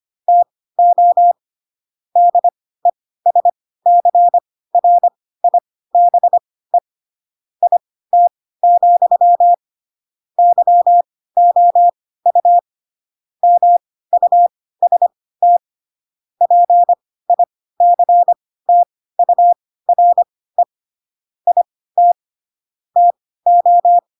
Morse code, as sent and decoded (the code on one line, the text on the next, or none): - --- / -.. ... -.-. .-. .. -... / .. - --..-- / -.-- --- ..- / -- ..- ... - / .--. .. -.-. - ..- .-. / .. - / - ---
TO DESCRIBE IT, YOU MUST PICTURE IT TO